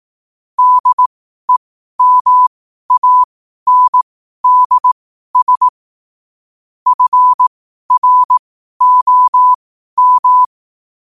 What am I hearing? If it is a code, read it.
Morse code, heard as a keyed tone, about 18 wpm, spaced as for 12 wpm, 1000 Hz. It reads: DEMANDS FROM
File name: qotdfiles / 12wpm